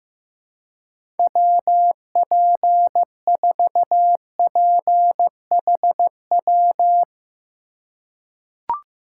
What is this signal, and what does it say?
Morse code, heard as a keyed tone, about 16 wpm, 700 Hz. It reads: WP4PHW
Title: USA Call Signs - 4x Character Spacing - 15wpm